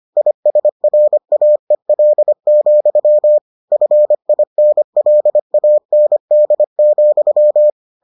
Morse code, heard as a keyed tone, about 25 wpm, 600 Hz. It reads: ISRAEL, FINLAND,